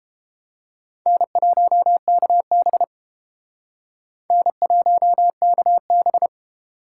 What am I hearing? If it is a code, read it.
Morse code, heard as a keyed tone, about 30 wpm, 700 Hz. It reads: N1KB N1KB